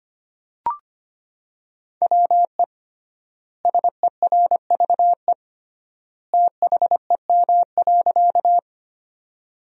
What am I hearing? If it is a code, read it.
Morse code, heard as a keyed tone, about 25 wpm, 700 Hz. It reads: WE SERVE THEM.
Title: 3x Word-Spacing - Sentences from Top 400 Words - 25wpm